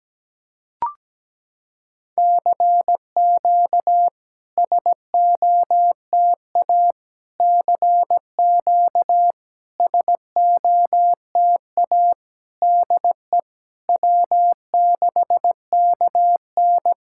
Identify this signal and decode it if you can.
Morse code, heard as a keyed tone, about 17 wpm, 700 Hz. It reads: CQ SOTA CQ SOTA DE W6KN
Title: SOTA QSO Practice 17wpm